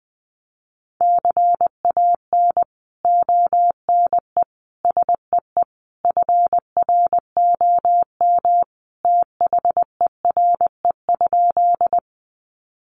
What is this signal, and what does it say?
Morse code, heard as a keyed tone, about 20 wpm, 700 Hz. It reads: CAN ONE SEE FROM THERE?
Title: Sentences from Top 100 Words 20wpm